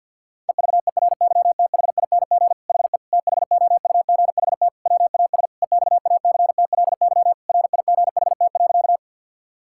Morse code, heard as a keyed tone, about 50 wpm, 700 Hz. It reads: EVERYTHING HE THOUGHT WAS EXACTLY RIGHT.
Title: Sentences from Top 1000 Words - Review Part 6 - 50wpm